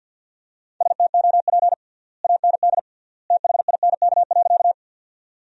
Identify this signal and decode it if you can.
Morse code, heard as a keyed tone, about 52 wpm, 700 Hz. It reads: STOP AND THINK.
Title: 3x Word-Spacing - Sentences from Top 300 Words - 50wpm